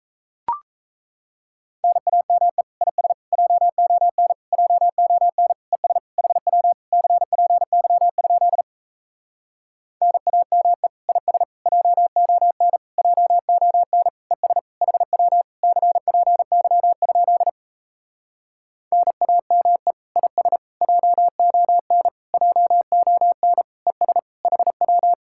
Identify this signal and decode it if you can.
Morse code, heard as a keyed tone, about 38 wpm, 700 Hz. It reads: NAME IS JON JON ES HW CPY? NAME IS JON JON ES HW CPY? NAME IS JON JON ES HW